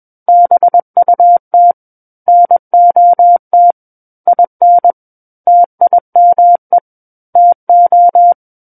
Morse code, heard as a keyed tone, about 21 wpm, 700 Hz. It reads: BUT NOT IN TIME TO
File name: Book One / 12-Chapter 12 - 21 WPM